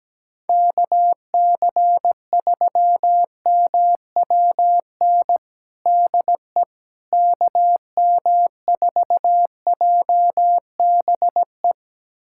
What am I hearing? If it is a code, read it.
Morse code, heard as a keyed tone, about 17 wpm, 700 Hz. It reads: KC3MWN DE KM4JBE
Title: USA Callsign DE Callsign - 6 Characters - 17wpm